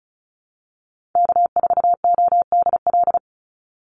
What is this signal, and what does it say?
Morse code, heard as a keyed tone, about 34 wpm, 700 Hz. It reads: K4ODL
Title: Call Signs Repeated 2x - 35wpm